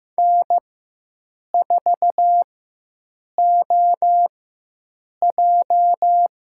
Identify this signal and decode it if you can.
Morse code, heard as a keyed tone, about 15 wpm, spaced as for 7 wpm, 700 Hz. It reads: N4OJ
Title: USA Call Signs - 4 Characters - 4x Character Spacing - 15wpm